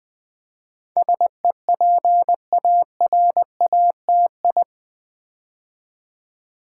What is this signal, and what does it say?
Morse code, heard as a keyed tone, about 20 wpm, 700 Hz. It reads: SEPARATI
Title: Build A Word - 4 Letters At A Time - 10-Letter Words - 20wpm